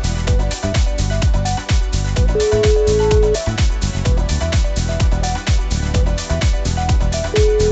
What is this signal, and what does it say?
Ringback tone (Europe, 425 Hz), with noise.